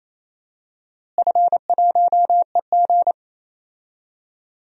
Morse code, heard as a keyed tone, about 28 wpm, 700 Hz. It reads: F1EG
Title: DXCC-100 Call Signs - 4 Characters - Repeated 2x - 28wpm